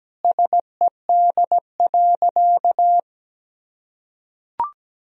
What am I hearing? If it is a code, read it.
Morse code, heard as a keyed tone, about 17 wpm, 700 Hz. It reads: SED.
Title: Sentences from Top 100 Words and Abbreviations 17wpm